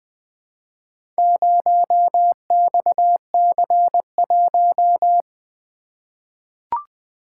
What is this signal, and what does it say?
Morse code, heard as a keyed tone, about 20 wpm, 700 Hz. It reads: 0XC1
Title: Four Letter-Number Combinations - 20wpm